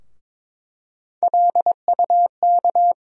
Morse code, heard as a keyed tone, about 22 wpm, 700 Hz. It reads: LUK
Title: USA Call Sign Postfixes - 3 Letters 22wpm